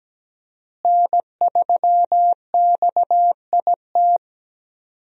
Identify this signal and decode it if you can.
Morse code, heard as a keyed tone, about 17 wpm, 700 Hz. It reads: N3XIT
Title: USA Call Signs - 5 Characters - Repeated 3x - 17wpm